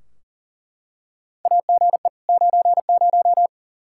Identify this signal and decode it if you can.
Morse code, heard as a keyed tone, about 40 wpm, 700 Hz. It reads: AGE 90
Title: K1USN SST QSO Practice 40wpm